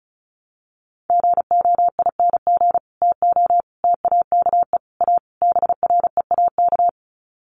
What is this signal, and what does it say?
Morse code, heard as a keyed tone, about 34 wpm, 700 Hz. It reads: GOING TO TAKE A BREAK
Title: Sets of 5 Words 35wpm